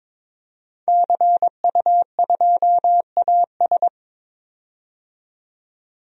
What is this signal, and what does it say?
Morse code, heard as a keyed tone, about 22 wpm, 700 Hz. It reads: CU2AS
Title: DXCC-100 Call Signs - 5 Characters - 22wpm